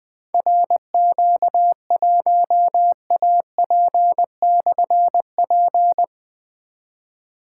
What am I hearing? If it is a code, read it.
Morse code, heard as a keyed tone, about 20 wpm, 700 Hz. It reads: RQ1AP/P
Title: Difficult Call Signs 20wpm